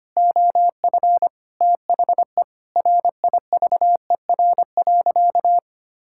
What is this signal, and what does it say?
Morse code, heard as a keyed tone, about 25 wpm, 700 Hz. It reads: OF THE RIVER.
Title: Sentences from Top 1000 Words - Review Part 6 - 25wpm